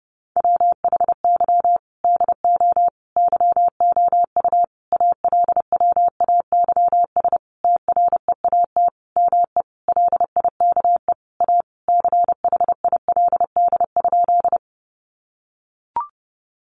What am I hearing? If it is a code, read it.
Morse code, heard as a keyed tone, about 28 wpm, 700 Hz. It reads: WHY DO YOU ALWAYS TREAT ME LIKE A CHILD?
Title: Sentences from Top 1300 Words 30wpm